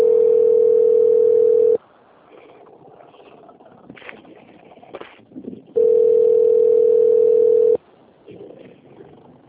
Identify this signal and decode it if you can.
Ringback tone (North America), 440 + 480 Hz.